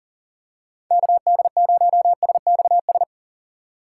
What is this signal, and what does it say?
Morse code, heard as a keyed tone, about 37 wpm, 700 Hz. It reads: KD0SXS
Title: USA Call Signs - 6 Characters - 2x Character Spacing - 40wpm